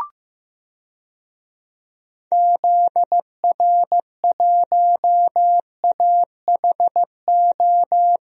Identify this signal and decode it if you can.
Morse code, heard as a keyed tone, about 16 wpm, 700 Hz. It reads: ZR1AHO
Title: DXCC-100 Call Signs - 6 Characters - Repeated 2x - 15wpm